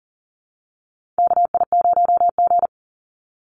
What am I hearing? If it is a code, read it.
Morse code, heard as a keyed tone, about 40 wpm, 700 Hz. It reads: KI0G
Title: NA SOTA Chaser Callsigns 40wpm